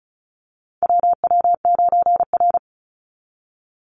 Morse code, heard as a keyed tone, about 34 wpm, 700 Hz. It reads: WW9R
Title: Call Signs Repeated 3x - 35wpm